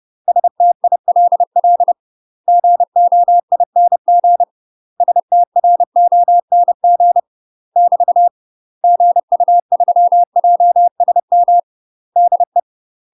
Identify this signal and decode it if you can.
Morse code, heard as a keyed tone, about 30 wpm, 700 Hz. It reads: STILL GOING STRONG <BT> GU3JSM DE
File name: UK MASTER QSOs TRAINING / 30WPM / UK MASTER QSOs 0022